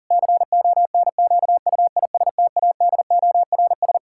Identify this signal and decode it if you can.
Morse code, heard as a keyed tone, about 40 wpm, 700 Hz. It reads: CONQUISTADORS
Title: Pangram Challenge v2 - 40wpm